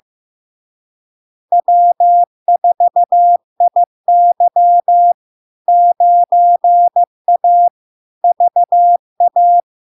Morse code, heard as a keyed tone, about 15 wpm, 700 Hz. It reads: W4IY 9A VA
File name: LARK-FD-2022-Log-in-CW-15wpm-700-Hz